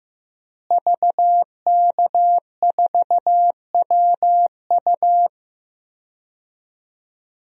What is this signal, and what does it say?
Morse code, heard as a keyed tone, about 15 wpm, 700 Hz. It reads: VK4WU
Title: DXCC-100 Call Signs - 5 Characters - Repeated 3x - 15wpm